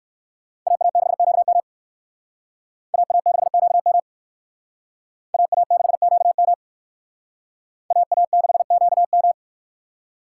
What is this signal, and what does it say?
Morse code, heard as a keyed tone, about 62 wpm, 690 Hz. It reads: AA6QM AA6QM AA6QM AA6QM